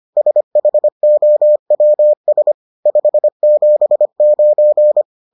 Morse code, heard as a keyed tone, about 25 wpm, 600 Hz. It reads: SHOWS 579